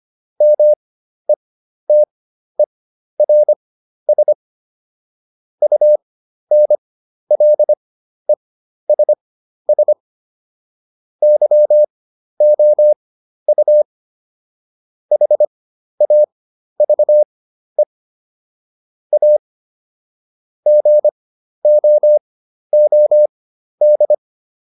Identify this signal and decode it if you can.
Morse code, heard as a keyed tone, about 25 wpm, spaced as for 12 wpm, 600 Hz. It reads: METERS UNLESS YOU HAVE A GOOD